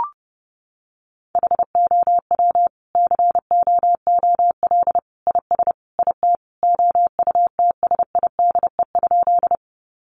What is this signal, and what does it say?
Morse code, heard as a keyed tone, about 29 wpm, 700 Hz. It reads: HOW COOL IS IT OUTSIDE?
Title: Sentences from Top 1200 Words 30wpm